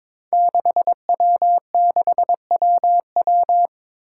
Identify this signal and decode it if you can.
Morse code, heard as a keyed tone, about 22 wpm, 700 Hz. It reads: 6W6WW
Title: Five Challenging Letter-Number Combinations 22wpm